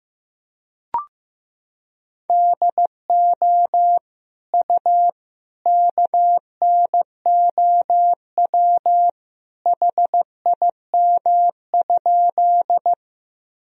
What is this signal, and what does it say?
Morse code, heard as a keyed tone, about 15 wpm, 700 Hz. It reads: DO U KNOW HIM?
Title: Sentences from Top 100 Words and Abbreviations 15wpm